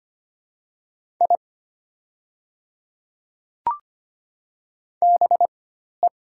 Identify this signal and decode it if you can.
Morse code, heard as a keyed tone, about 25 wpm, spaced as for 12 wpm, 700 Hz. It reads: I BE
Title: 4x Character Spacing - Top 100 Words - 25wpm